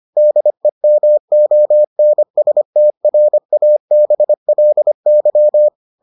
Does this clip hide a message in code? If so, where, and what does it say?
Yes: Morse code from 0.2 to 5.7 s, heard as a keyed tone, about 25 wpm, 600 Hz. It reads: DEMONSTRABLY